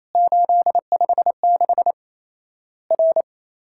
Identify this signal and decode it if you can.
Morse code, heard as a keyed tone, about 28 wpm, 700 Hz. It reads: 856 R
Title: Worked All Europe DX Contest - QTC Exchange - 28wpm